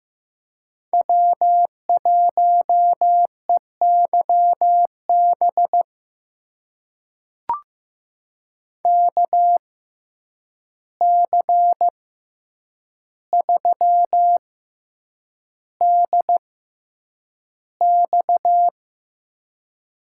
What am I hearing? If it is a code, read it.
Morse code, heard as a keyed tone, about 15 wpm, spaced as for 8 wpm, 700 Hz. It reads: W1EYB KC3DX
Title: USA Call Signs - 5 Characters - 6x Character Spacing - 15wpm